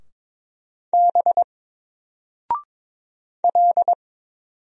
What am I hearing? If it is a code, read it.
Morse code, heard as a keyed tone, about 22 wpm, 700 Hz. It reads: B L
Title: Single Letter-Number 22wpm